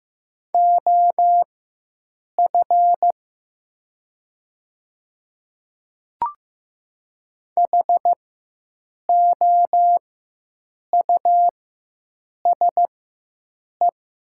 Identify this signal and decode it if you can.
Morse code, heard as a keyed tone, about 15 wpm, spaced as for 7 wpm, 700 Hz. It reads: OF HOUSE
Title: 4x Character Spacing - Top 200 Words - 15wpm